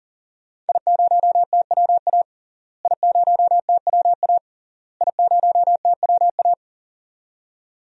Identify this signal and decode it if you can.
Morse code, heard as a keyed tone, about 40 wpm, 700 Hz. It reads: I0TWA I0TWA I0TWA
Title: DXCC-100 Call Signs - 5 Characters - Repeated 3x - 40wpm